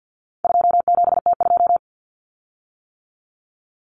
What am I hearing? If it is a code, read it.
Morse code, heard as a keyed tone, about 50 wpm, 700 Hz. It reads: 27T2
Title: Common Cut Numbers - 4-Digit 50wpm